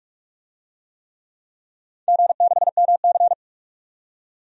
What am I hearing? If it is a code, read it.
Morse code, heard as a keyed tone, about 44 wpm, 700 Hz. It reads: G6MC
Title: DXCC-100 Call Signs - 4 Characters - 45wpm